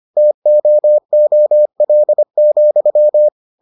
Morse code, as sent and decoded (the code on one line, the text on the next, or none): - --- --- .-.. --..--
TOOL,